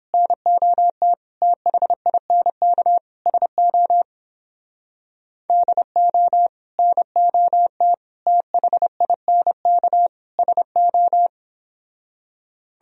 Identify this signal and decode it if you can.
Morse code, heard as a keyed tone, about 31 wpm, 700 Hz. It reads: NOT THINK SO DO NOT THINK SO